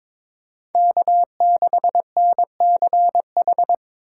Morse code, heard as a keyed tone, about 26 wpm, 700 Hz. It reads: K6NCH
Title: Speed Racing - USA Call Signs 22wpm